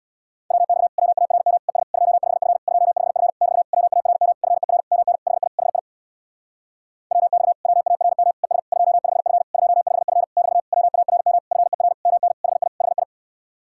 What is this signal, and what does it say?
Morse code, heard as a keyed tone, about 66 wpm, 700 Hz. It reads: WX WARM ES 25F 25F <BT> WARM FER MN HI HI WX WARM ES 25F 25F <BT> WARM FER MN HI HI